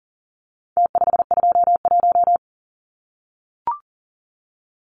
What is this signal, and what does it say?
Morse code, heard as a keyed tone, about 40 wpm, 700 Hz. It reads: T521
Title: Numbers - Contest Style - 4-Digit 40wpm